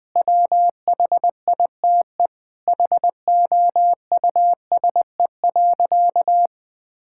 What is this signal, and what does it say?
Morse code, heard as a keyed tone, about 20 wpm, 700 Hz. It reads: WHITE HOUSE.
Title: Sentences with 8-Word Phrases - Spoken Beginning - 20wpm